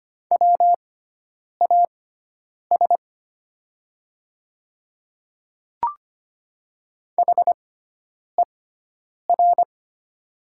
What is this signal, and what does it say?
Morse code, heard as a keyed tone, about 25 wpm, spaced as for 9 wpm, 700 Hz. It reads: WAS HER
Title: 6x Character Spacing - Top 100 Words - 25wpm